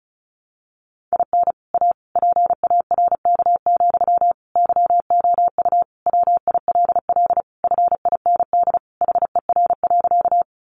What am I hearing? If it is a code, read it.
Morse code, heard as a keyed tone, about 34 wpm, 700 Hz. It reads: IN A PARK, YOU WILL FIND HER.
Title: Sentences from Top 1300 Words 35wpm